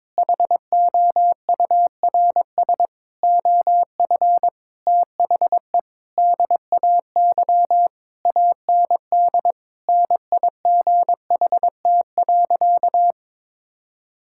Morse code, heard as a keyed tone, about 21 wpm, 700 Hz. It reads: HOURS OF THE DAY AND NIGHT.
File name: practice / 22wpm / Sentences with 8-Word Phrases - 22wpm